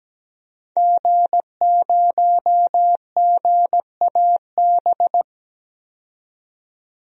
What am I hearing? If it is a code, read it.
Morse code, heard as a keyed tone, about 17 wpm, 700 Hz. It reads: G0GAB
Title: DXCC-100 Call Signs - 5 Characters - 17wpm